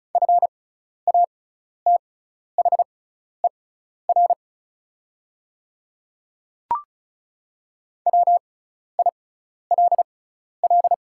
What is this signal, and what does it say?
Morse code, heard as a keyed tone, about 36 wpm, spaced as for 12 wpm, 690 Hz. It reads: FATHER WILL
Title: 6x Character Spacing - Top 200 Words - 35wpm